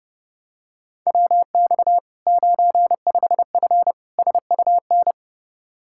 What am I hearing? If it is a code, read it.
Morse code, heard as a keyed tone, about 30 wpm, 700 Hz. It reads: WX 95F SUN
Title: Practice QSO - Exchange 2 - Info - 30wpm